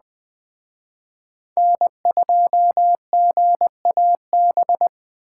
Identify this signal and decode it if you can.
Morse code, heard as a keyed tone, about 20 wpm, 700 Hz. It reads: N2GAB